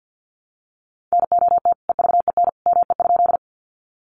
Morse code, heard as a keyed tone, about 50 wpm, 700 Hz. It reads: NOT EVEN ME?
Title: Sentences from Top 600 Words 50wpm